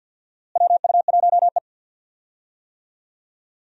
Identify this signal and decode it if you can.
Morse code, heard as a keyed tone, about 51 wpm, 700 Hz. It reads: WU1E
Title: USA Call Signs - 4 Characters - Repeated 3x - 50wpm